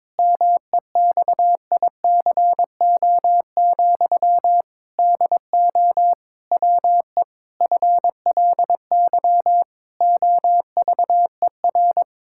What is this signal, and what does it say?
Morse code, heard as a keyed tone, about 22 wpm, 700 Hz. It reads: MEXICO, DO WE FLY OVER
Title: Pangram Challenge v2 - 22wpm